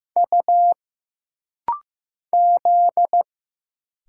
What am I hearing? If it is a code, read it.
Morse code, heard as a keyed tone, about 15 wpm, 700 Hz. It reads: U Z
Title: Single Letters - Rapid-Fire 15wpm